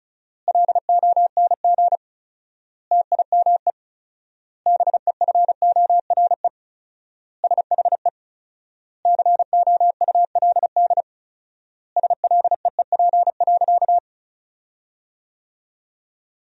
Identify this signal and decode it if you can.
Morse code, heard as a keyed tone, about 36 wpm, 700 Hz. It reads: LONG TIME BEFORE SHE COULD SLEEP.
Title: 4x Word-Spacing - Sentences from Top 500 Words - 35wpm